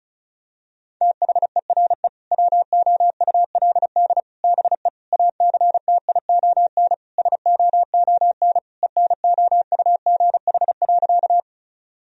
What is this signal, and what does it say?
Morse code, heard as a keyed tone, about 36 wpm, 700 Hz. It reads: THERE WOULD BE ACTION SOON ENOUGH.
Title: Sentences from Top 1000 Words - Review Part 4 - 35wpm